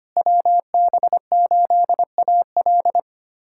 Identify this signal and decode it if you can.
Morse code, heard as a keyed tone, about 25 wpm, 700 Hz. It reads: WB8AL
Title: USA Callsign DE Callsign 25wpm